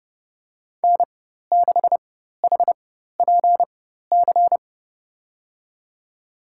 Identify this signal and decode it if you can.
Morse code, heard as a keyed tone, about 30 wpm, spaced as for 14 wpm, 700 Hz. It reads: N6HPC
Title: USA Call Signs - 5 Characters - 4x Character Spacing - 30wpm